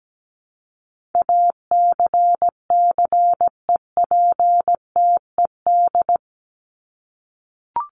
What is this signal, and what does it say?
Morse code, heard as a keyed tone, about 17 wpm, 700 Hz. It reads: ACCEPTED
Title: ED Suffix - Top 100 Words with Base Words 17wpm